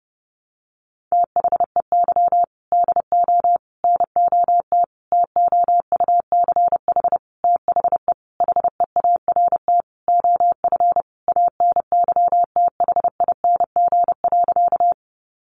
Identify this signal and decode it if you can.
Morse code, heard as a keyed tone, about 30 wpm, 700 Hz. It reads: THEY DO NOT TOUCH THE HEART OF ANYTHING.
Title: Sentences from Top 1200 Words 30wpm